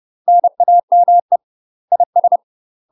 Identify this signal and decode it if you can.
Morse code, heard as a keyed tone, about 30 wpm, 700 Hz. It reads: NAME IS
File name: UK MASTER QSOs TRAINING / 30WPM / UK MASTER QSOs 0039